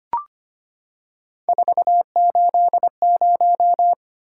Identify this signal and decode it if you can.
Morse code, heard as a keyed tone, about 25 wpm, 700 Hz. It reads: 480
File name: Three Letter-Number Combinations - 25wpm